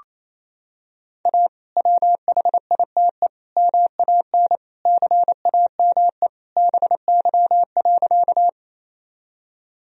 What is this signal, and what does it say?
Morse code, heard as a keyed tone, about 28 wpm, 700 Hz. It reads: A WHITE MAN CAME BY.
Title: Sentences from Top 1000 Words - Review Part 1 - 28wpm